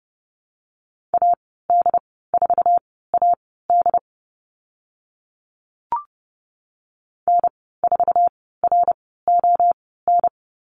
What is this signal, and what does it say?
Morse code, heard as a keyed tone, about 30 wpm, spaced as for 17 wpm, 690 Hz. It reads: AD4AD N4RON
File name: Lesson 15g - Number 5 - Call Signs 30wpm